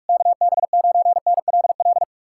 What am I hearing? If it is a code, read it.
Morse code, heard as a keyed tone, about 59 wpm, spaced as for 52 wpm, 700 Hz. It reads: KB9NLL
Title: Speed-Racing Common QSO Phrases - 45wpm